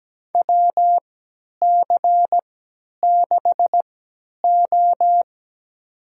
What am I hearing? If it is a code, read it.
Morse code, heard as a keyed tone, about 17 wpm, spaced as for 10 wpm, 700 Hz. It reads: WC6O
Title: USA Call Signs - 3x Character Spacing - 17wpm